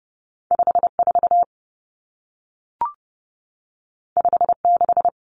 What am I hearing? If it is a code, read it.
Morse code, heard as a keyed tone, about 30 wpm, 700 Hz. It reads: 54 56
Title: Common Cut Numbers - 2-Digit 30wpm